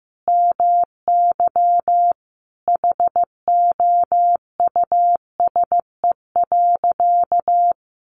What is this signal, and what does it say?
Morse code, heard as a keyed tone, about 15 wpm, 700 Hz. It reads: MY HOUSE.
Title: Sentences from Top 600 Words 15wpm